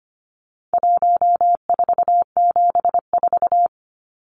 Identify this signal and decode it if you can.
Morse code, heard as a keyed tone, about 25 wpm, 700 Hz. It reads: 1474
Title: Numbers - 4-Digit 25wpm